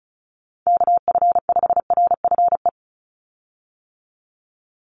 Morse code, heard as a keyed tone, about 36 wpm, 700 Hz. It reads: KF5RFE
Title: USA Call Signs 35wpm